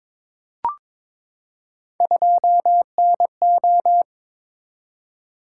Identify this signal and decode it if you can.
Morse code, heard as a keyed tone, about 22 wpm, 700 Hz. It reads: 2NO
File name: Three Letter-Number Combinations - 22wpm